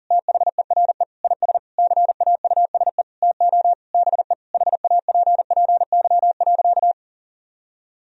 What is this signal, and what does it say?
Morse code, heard as a keyed tone, about 40 wpm, 700 Hz. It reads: THERE IS CAUSE TO BE HAPPY.
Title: Sentences from Top 1000 Words - Review Part 1 - 40wpm